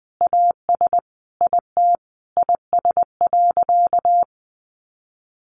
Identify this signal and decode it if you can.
Morse code, heard as a keyed tone, about 19 wpm, 700 Hz. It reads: AS IT IS.
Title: Sentences from Top 1200 Words 20wpm